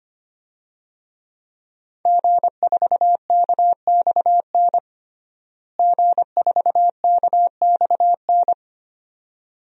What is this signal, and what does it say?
Morse code, heard as a keyed tone, about 25 wpm, 700 Hz. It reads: G4KXN G4KXN
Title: DXCC-100 Call Signs - 5 Characters - Repeated 2x - 25wpm